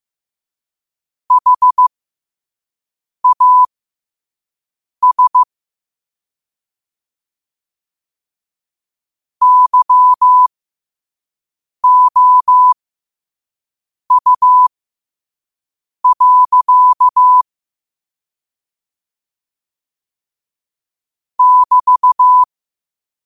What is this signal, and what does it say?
Morse code, heard as a keyed tone, about 15 wpm, spaced as for 5 wpm, 1000 Hz. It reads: HAS YOU. <BT>
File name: qotdfiles / 5wpm